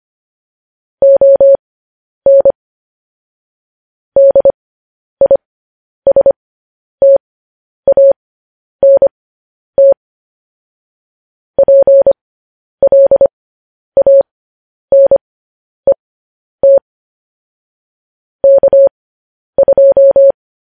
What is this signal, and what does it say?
Morse code, heard as a keyed tone, about 25 wpm, spaced as for 10 wpm, 550 Hz. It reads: ON DISTANT PLANET K2
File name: podcast / News headlines-1x-25x10